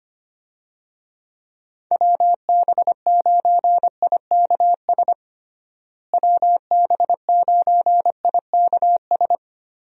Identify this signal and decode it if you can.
Morse code, heard as a keyed tone, about 25 wpm, 700 Hz. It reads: WB9IKS WB9IKS
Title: USA Call Signs - 6 Characters - Repeated 2x - 25wpm